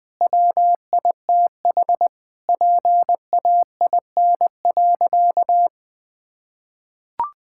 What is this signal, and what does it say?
Morse code, heard as a keyed tone, about 20 wpm, 700 Hz. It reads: WITH PAIN.
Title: Sentences from Top 1600 Words 20wpm